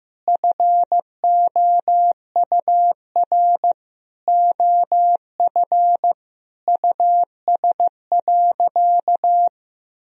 Morse code, heard as a keyed tone, about 15 wpm, 700 Hz. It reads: FOUR OF US.
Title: Sentences from Top 300 Words and Abbreviations 15wpm